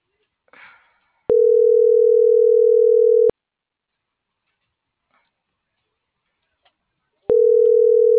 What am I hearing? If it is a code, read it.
Ringback tone (North America), 440 + 480 Hz.